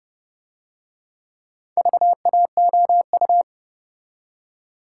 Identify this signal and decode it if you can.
Morse code, heard as a keyed tone, about 29 wpm, 700 Hz. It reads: VAOU